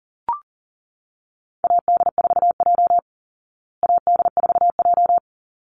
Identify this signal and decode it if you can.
Morse code, heard as a keyed tone, about 40 wpm, 700 Hz. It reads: AD4J AD4J
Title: Call Signs Repeated 3x - 40wpm